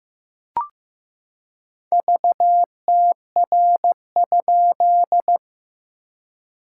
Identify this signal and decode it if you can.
Morse code, heard as a keyed tone, about 15 wpm, 690 Hz. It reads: VTR?
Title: SOTA QSO Practice 15wpm